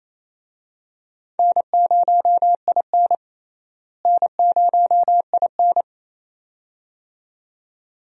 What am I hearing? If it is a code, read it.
Morse code, heard as a keyed tone, about 28 wpm, 700 Hz. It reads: N0IN N0IN
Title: USA Call Signs - 4 Characters - Repeated 2x - 28wpm